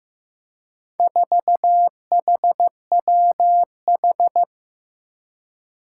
Morse code, heard as a keyed tone, about 15 wpm, 700 Hz. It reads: 4HWH
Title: Four Challenging Letter-Number Combinations 15wpm